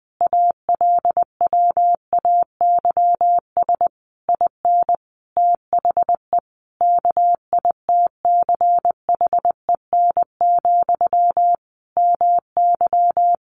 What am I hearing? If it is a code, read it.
Morse code, heard as a keyed tone, about 20 wpm, 700 Hz. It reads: ALWAYS IN THE KITCHEN, MY
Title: Sentences from Top 1400 Words 20wpm